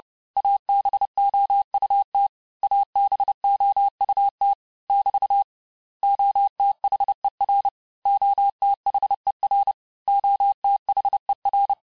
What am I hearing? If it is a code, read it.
Morse code, heard as a keyed tone, about 30 wpm, 800 Hz. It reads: ABOUT ABOUT <BT> OTHER OTHER OTHER
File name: cours-du-mercredi / audios / cours-854-mots-vitesse-28-mpm